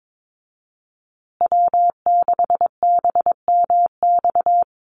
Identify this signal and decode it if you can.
Morse code, heard as a keyed tone, about 22 wpm, 690 Hz. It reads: W6BMX
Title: USA Call Signs Repeated 3x 22wpm